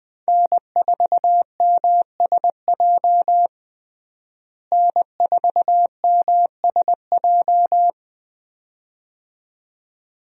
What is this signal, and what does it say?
Morse code, heard as a keyed tone, about 20 wpm, 700 Hz. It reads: N4MSJ N4MSJ
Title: USA Call Signs - 5 Characters - Repeated 2x - 20wpm